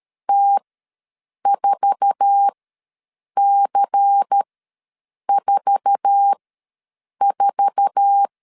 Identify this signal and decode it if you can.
Morse code, heard as a keyed tone, about 13 wpm, spaced as for 7 wpm, 800 Hz. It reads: T4C44
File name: LEZIONI / 8 YJ45 / 8 YJ4 ws 12wpm 3249